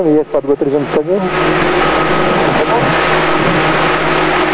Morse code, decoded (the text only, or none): none